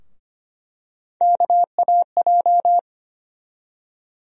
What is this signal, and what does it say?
Morse code, heard as a keyed tone, about 25 wpm, 700 Hz. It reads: KAJ
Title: USA Call Sign Postfixes - 3 Letters 25wpm